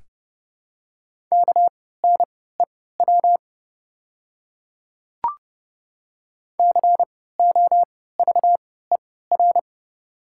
Morse code, heard as a keyed tone, about 29 wpm, spaced as for 17 wpm, 700 Hz. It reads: KNEW COVER
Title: 3x Character Spacing - Top 500 Words - 30wpm